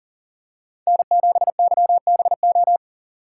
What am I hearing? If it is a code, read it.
Morse code, heard as a keyed tone, about 40 wpm, 700 Hz. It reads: N7YBO
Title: USA Call Signs - 5 Characters - 2x Character Spacing - 40wpm